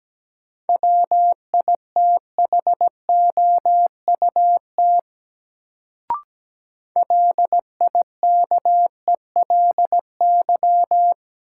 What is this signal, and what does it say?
Morse code, heard as a keyed tone, about 17 wpm, 700 Hz. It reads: WITHOUT LIKELY
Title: Sentences from Top 600 Words - New Word Drill - 17wpm